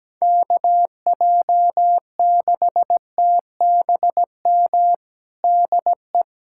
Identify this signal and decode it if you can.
Morse code, heard as a keyed tone, about 17 wpm, 700 Hz. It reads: KJ6TBM DE
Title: USA Callsign DE Callsign 17wpm